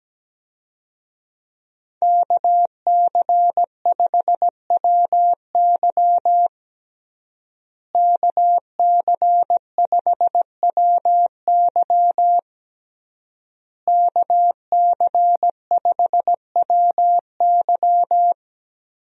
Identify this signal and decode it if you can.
Morse code, heard as a keyed tone, about 17 wpm, 700 Hz. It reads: KC5WY KC5WY KC5WY
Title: USA Call Signs - 5 Characters - Repeated 3x - 17wpm